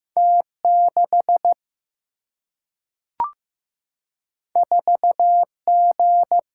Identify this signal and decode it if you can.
Morse code, heard as a keyed tone, about 15 wpm, 700 Hz. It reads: T6 4G
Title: Single-Letter Single-Number 15wpm